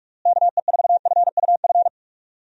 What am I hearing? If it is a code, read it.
Morse code, heard as a keyed tone, about 46 wpm, 700 Hz. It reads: KE4FUF
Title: USA Call Signs - 6 Characters - Repeated 2x - 45wpm